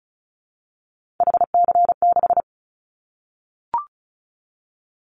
Morse code, heard as a keyed tone, about 34 wpm, 700 Hz. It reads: HC6
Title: Callsign Prefixes for Top 100 DXCC - 35wpm